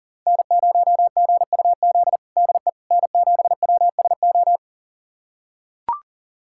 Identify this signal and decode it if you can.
Morse code, heard as a keyed tone, about 40 wpm, 700 Hz. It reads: N0GUZ DE N7WSO
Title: USA Callsign DE Callsign - 5 Characters - 40wpm